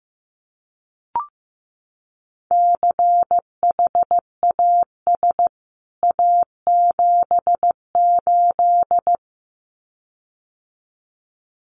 Morse code, heard as a keyed tone, about 15 wpm, 700 Hz. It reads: CHAS A78
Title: CWT Exchange 15wpm